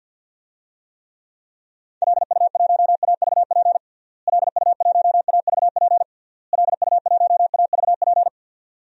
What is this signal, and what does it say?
Morse code, heard as a keyed tone, about 47 wpm, 700 Hz. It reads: LU1AVP LU1AVP LU1AVP
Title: DXCC-100 Call Signs - 6 Characters - Repeated 3x - 50wpm